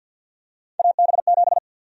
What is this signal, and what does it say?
Morse code, heard as a keyed tone, about 51 wpm, 700 Hz. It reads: AB7
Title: USA Call Sign Prefixes - 50wpm